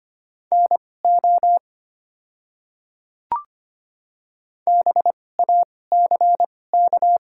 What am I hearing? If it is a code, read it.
Morse code, heard as a keyed tone, about 25 wpm, spaced as for 18 wpm, 700 Hz. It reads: NO BACK
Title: 2x Character Spacing - Top 200 Words - 25wpm